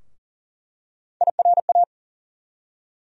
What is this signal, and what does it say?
Morse code, heard as a keyed tone, about 40 wpm, 700 Hz. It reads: IRA
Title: USA Call Sign Postfixes - 3 Letters 40wpm